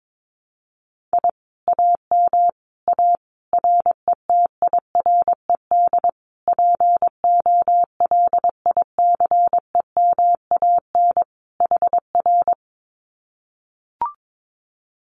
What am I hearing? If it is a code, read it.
Morse code, heard as a keyed tone, about 22 wpm, 700 Hz. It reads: I AM A RETIRED POLICEMAN HR
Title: Common QSO Phrases Encore 22wpm